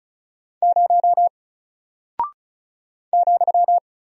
Morse code, heard as a keyed tone, about 36 wpm, 700 Hz. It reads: 0 ,
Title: Single Character (Letter:Number:Punctuation:Prosign) 35wpm